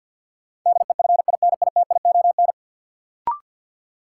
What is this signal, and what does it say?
Morse code, heard as a keyed tone, about 51 wpm, 700 Hz. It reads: DEFINITION
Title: Sentences from Top 1000 Words - Review Part 4 - New Word Drill - 50wpm